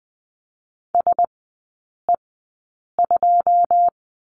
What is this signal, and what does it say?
Morse code, heard as a keyed tone, about 20 wpm, spaced as for 8 wpm, 700 Hz. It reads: SE2
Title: 081 Lesson 14e - Number 2 - Three Characters 20wpm